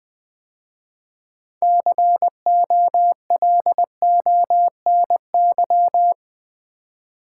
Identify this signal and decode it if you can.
Morse code, heard as a keyed tone, about 20 wpm, 700 Hz. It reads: COLONY
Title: 6 Letter Words - First 3 Letters Given - 20wpm